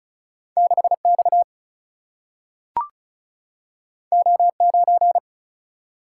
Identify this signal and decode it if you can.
Morse code, heard as a keyed tone, about 35 wpm, 700 Hz. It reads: 6X O9
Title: Single-Letter Single-Number 35wpm